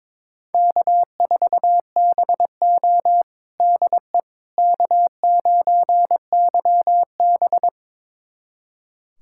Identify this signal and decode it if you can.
Morse code, heard as a keyed tone, about 22 wpm, 700 Hz. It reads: K4BO DE K9YB
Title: USA Callsign DE Callsign - 4 Characters - 22wpm